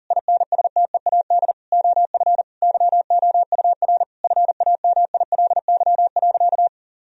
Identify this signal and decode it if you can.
Morse code, heard as a keyed tone, about 39 wpm, 700 Hz. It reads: INSTEAD OF YOUR FAMILY.
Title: Sentences from Top 1000 Words - Review Part 4 - 40wpm